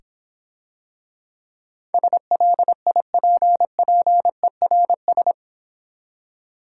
Morse code, heard as a keyed tone, about 26 wpm, 700 Hz. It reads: SLIPPERS